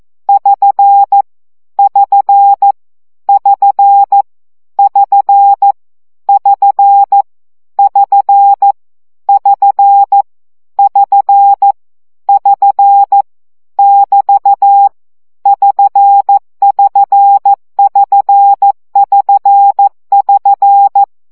Morse code, heard as a keyed tone, about 14 wpm, 800 Hz. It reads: <SN> <SN> <SN> <SN> <SN> <SN> <SN> <SN> <SN> <BT> <SN><SN><SN><SN><SN>
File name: CW gyakorlas sn 12 baud